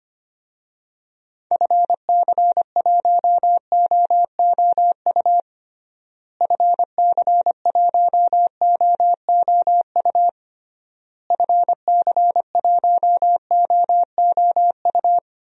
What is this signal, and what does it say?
Morse code, heard as a keyed tone, about 25 wpm, 700 Hz. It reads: FC1OOU FC1OOU FC1OOU
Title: DXCC-100 Call Signs - 6 Characters - Repeated 3x - 25wpm